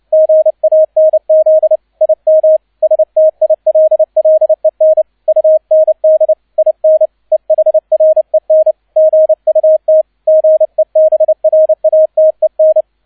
Morse code, heard as a keyed tone, about 29 wpm, 630 Hz. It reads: GANZ IM STILLEN UND IN EHREN GUT GEBRATEN
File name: 002 Zweiter Streich 30wpm